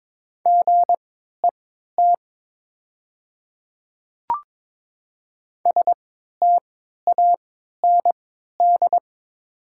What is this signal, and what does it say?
Morse code, heard as a keyed tone, about 22 wpm, spaced as for 13 wpm, 700 Hz. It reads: GET STAND
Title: 3x Character Spacing - Top 300 Words - 22wpm